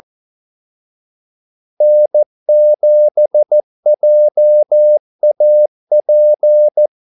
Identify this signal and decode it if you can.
Morse code, heard as a keyed tone, about 14 wpm, 600 Hz. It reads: N7JAP
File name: Callsigns-14wpm